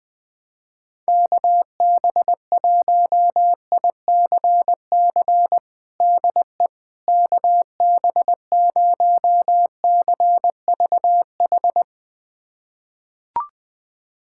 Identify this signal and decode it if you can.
Morse code, heard as a keyed tone, about 21 wpm, 700 Hz. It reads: KB1ICC DE KB0CVH
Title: USA Callsign DE Callsign 20wpm